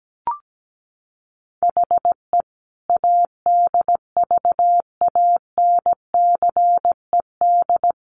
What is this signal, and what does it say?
Morse code, heard as a keyed tone, about 17 wpm, 700 Hz. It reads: HE ADVANCED
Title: Sentences from Top 1400 Words 17wpm